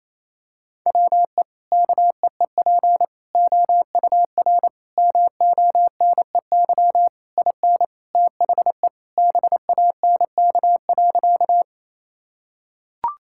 Morse code, heard as a keyed tone, about 28 wpm, 700 Hz. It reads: WE KEEP OUR MONEY IN THE BANK.
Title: Sentences from Top 1000 Words - Review Part 6 - 28wpm